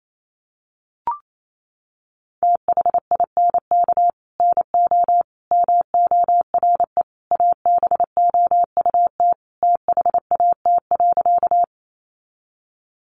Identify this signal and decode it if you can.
Morse code, heard as a keyed tone, about 28 wpm, 700 Hz. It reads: THINK NO MORE ABOUT THAT.
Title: Sentences from Top 200 Words 28wpm